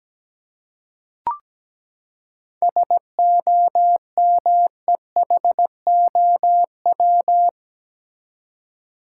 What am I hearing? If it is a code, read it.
Morse code, heard as a keyed tone, about 17 wpm, 700 Hz. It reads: SOMEHOW
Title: Seven Letter Words - 17wpm